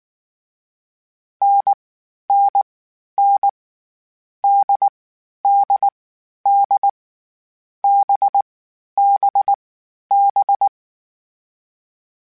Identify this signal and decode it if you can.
Morse code, heard as a keyed tone, about 19 wpm, 800 Hz. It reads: N N N D D D B B B
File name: Short Runs / Short Runs-mp3s / NDB NDB6 @18